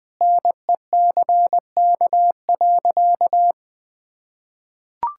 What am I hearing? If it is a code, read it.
Morse code, heard as a keyed tone, about 20 wpm, 700 Hz. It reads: NECK.
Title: Sentences with 4-Word Phrases - 20wpm